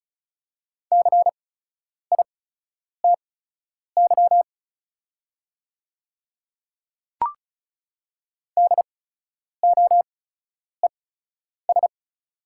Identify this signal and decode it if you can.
Morse code, heard as a keyed tone, about 36 wpm, spaced as for 10 wpm, 690 Hz. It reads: CITY DOES
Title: 8x Character Spacing - Top 300 Words - 35wpm